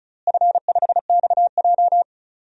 Morse code, heard as a keyed tone, about 36 wpm, 700 Hz. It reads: F5XJ
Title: DXCC-100 Call Signs - 4 Characters - Repeated 2x - 35wpm